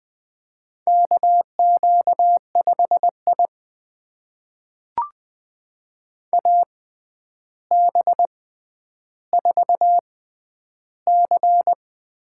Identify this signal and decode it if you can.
Morse code, heard as a keyed tone, about 20 wpm, spaced as for 10 wpm, 700 Hz. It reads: KQ5I AB4C
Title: USA Call Signs - 4 Characters - 6x Character Spacing - 20wpm